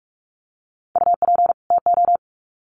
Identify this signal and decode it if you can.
Morse code, heard as a keyed tone, about 44 wpm, 700 Hz. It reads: UP TO
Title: Sets of 2 Words 45wpm